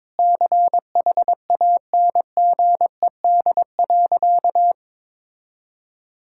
Morse code, heard as a keyed tone, about 22 wpm, 700 Hz. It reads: CHANGED.
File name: practice / 22wpm / Sentences with 4-Word Phrases - 22wpm